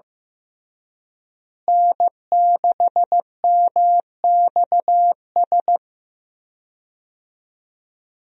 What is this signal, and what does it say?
Morse code, heard as a keyed tone, about 16 wpm, 700 Hz. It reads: N6MXS
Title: USA Call Signs - 5 Characters - Repeated 3x - 15wpm